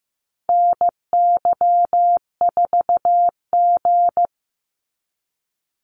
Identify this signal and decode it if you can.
Morse code, heard as a keyed tone, about 15 wpm, 700 Hz. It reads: NY4G
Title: NA SOTA Activator Callsigns 15wpm